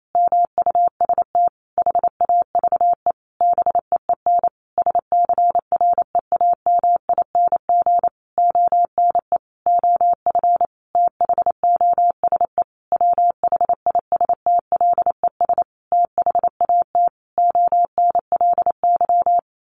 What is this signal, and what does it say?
Morse code, heard as a keyed tone, about 28 wpm, 700 Hz. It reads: MUST HAVE BEEN SCREAMING ONE OF THOSE WHISTLES THAT ONLY
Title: A Sound Of Thunder 28wpm